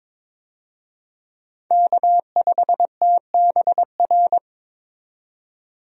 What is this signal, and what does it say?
Morse code, heard as a keyed tone, about 22 wpm, 700 Hz. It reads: K5TBR
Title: USA Call Signs - 5 Characters - 22wpm